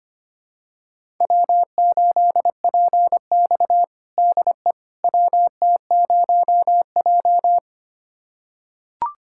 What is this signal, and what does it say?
Morse code, heard as a keyed tone, about 25 wpm, 700 Hz. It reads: W8PX DE WT0J
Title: USA Callsign DE Callsign - 4 Characters - 25wpm